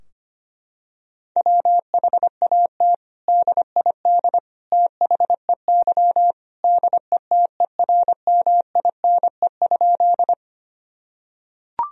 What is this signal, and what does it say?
Morse code, heard as a keyed tone, about 25 wpm, 700 Hz. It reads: WHAT DID THEY DETERMINE?
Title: Sentences from Top 1000 Words - Review Part 4 - 25wpm